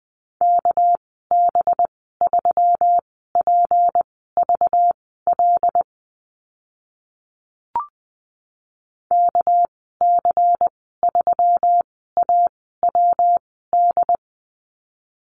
Morse code, heard as a keyed tone, about 20 wpm, spaced as for 14 wpm, 700 Hz. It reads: KB3PVL KC3AWD
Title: 200 Lesson 32g - Letter k - Call Signs 20wpm